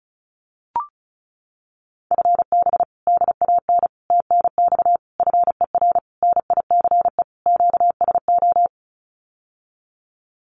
Morse code, heard as a keyed tone, about 34 wpm, 700 Hz. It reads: FB DAN TNX FER NICE QSO
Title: Common QSO Phrases Encore 35wpm